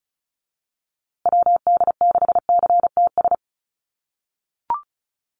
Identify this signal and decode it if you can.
Morse code, heard as a keyed tone, about 34 wpm, 700 Hz. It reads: WD6CTS
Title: USA Call Signs Repeated 3x 35wpm